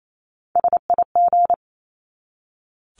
Morse code, heard as a keyed tone, about 28 wpm, 700 Hz. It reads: SIG
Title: Abbreviations 28wpm